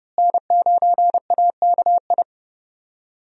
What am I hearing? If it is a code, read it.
Morse code, heard as a keyed tone, about 30 wpm, 700 Hz. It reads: N9AKI
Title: USA Call Signs - 5 Characters - Repeated 3x - 30wpm